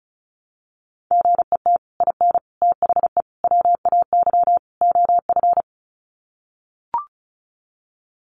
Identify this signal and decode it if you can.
Morse code, heard as a keyed tone, about 34 wpm, 700 Hz. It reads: GET IN THE WAY OF
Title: Sets of 5 Words Encore 35wpm